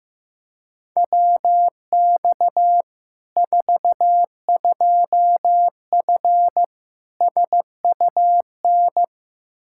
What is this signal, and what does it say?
Morse code, heard as a keyed tone, about 15 wpm, 700 Hz. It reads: WX 42F SUN
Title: K1USN SST QSO Practice 15wpm